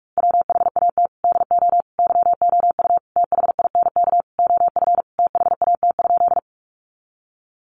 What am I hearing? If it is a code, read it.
Morse code, heard as a keyed tone, about 44 wpm, 700 Hz. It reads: WHAT DO YOU THINK OF THAT?
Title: Sentences from Top 200 Words 45wpm